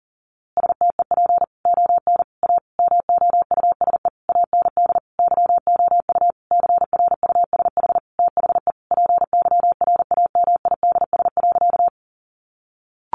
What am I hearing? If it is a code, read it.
Morse code, heard as a keyed tone, about 37 wpm, 700 Hz. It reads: STEP ON A MOUSE AND YOU CRUSH THE PYRAMIDS.
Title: A Sound Of Thunder 40wpm